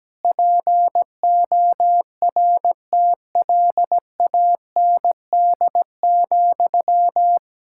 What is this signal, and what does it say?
Morse code, heard as a keyed tone, about 17 wpm, 700 Hz. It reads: PORTLAND,
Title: Practice QSO - Exchange 1 - Beginning - 17wpm